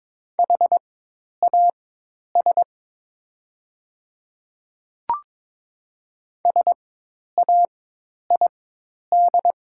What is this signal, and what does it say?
Morse code, heard as a keyed tone, about 22 wpm, spaced as for 10 wpm, 700 Hz. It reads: HAS SAID
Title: 4x Character Spacing - Top 100 Words - 22wpm